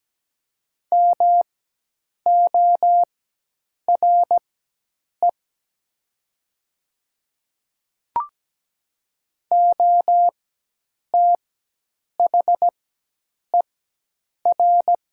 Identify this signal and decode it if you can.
Morse code, heard as a keyed tone, about 17 wpm, spaced as for 8 wpm, 700 Hz. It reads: MORE OTHER
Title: 4x Character Spacing - Top 100 Words - 17wpm